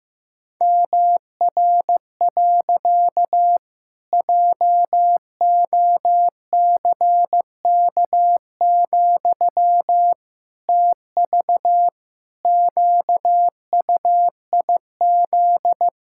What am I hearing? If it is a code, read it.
Morse code, heard as a keyed tone, about 15 wpm, 700 Hz. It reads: MR. JOCK, TV QUIZ
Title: Pangram Challenge v2 - 15wpm